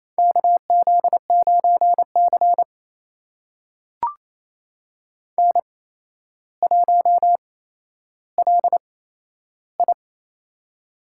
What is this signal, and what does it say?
Morse code, heard as a keyed tone, about 28 wpm, spaced as for 12 wpm, 700 Hz. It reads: KZ9C N1LI
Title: USA Call Signs - 4 Characters - 8x Character Spacing - 28wpm